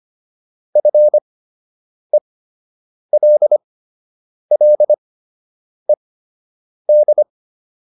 Morse code, heard as a keyed tone, about 25 wpm, spaced as for 8 wpm, 600 Hz. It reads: FELLED
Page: fundamental-practice-files